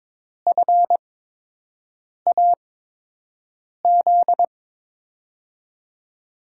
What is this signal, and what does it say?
Morse code, heard as a keyed tone, about 22 wpm, spaced as for 6 wpm, 700 Hz. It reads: FAZ
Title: USA Call Signs - 8x Character Spacing - 22wpm